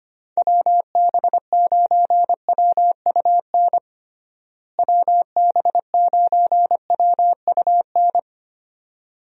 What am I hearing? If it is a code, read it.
Morse code, heard as a keyed tone, about 25 wpm, 700 Hz. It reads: WB9WUN WB9WUN
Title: USA Call Signs - 6 Characters - Repeated 2x - 25wpm